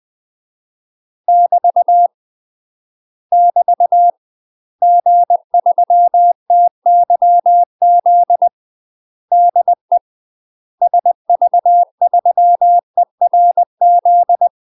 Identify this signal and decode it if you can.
Morse code, heard as a keyed tone, about 20 wpm, 700 Hz. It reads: <BT> <BT> G3TYZ DE SV3ERZ
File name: UK MASTER QSOs TRAINING / 20WPM / UK MASTER QSOs 0068